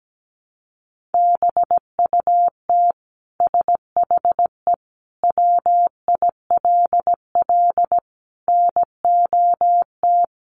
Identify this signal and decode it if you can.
Morse code, heard as a keyed tone, about 17 wpm, 700 Hz. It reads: BUT SHE WILL NOT
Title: Sentences from Top 1200 Words 17wpm